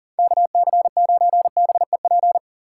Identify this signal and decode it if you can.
Morse code, heard as a keyed tone, about 51 wpm, 700 Hz. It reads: KC9BEP